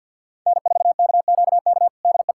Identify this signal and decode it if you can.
Morse code, heard as a keyed tone, about 50 wpm, 700 Hz. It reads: N4KQK DE
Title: USA Callsign DE Callsign - 5 Characters - 50wpm